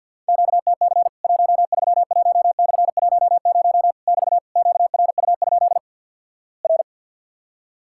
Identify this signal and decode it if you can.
Morse code, heard as a keyed tone, about 50 wpm, 700 Hz. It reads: QTC 131/10 <BT> QRU? R
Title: Worked All Europe DX Contest - QTC Exchange - 50wpm